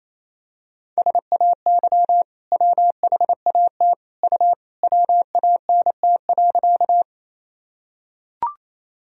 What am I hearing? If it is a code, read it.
Morse code, heard as a keyed tone, about 28 wpm, 700 Hz. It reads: SAY WHAT U WANT.
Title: Sentences from Top 200 Words and Abbreviations 28wpm